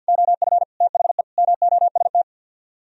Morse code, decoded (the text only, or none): OF THE MOST